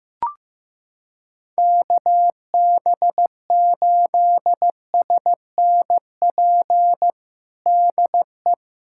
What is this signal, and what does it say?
Morse code, heard as a keyed tone, about 15 wpm, 700 Hz. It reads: KB8SNP DE
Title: USA Callsign DE Callsign - 6 Characters - 15wpm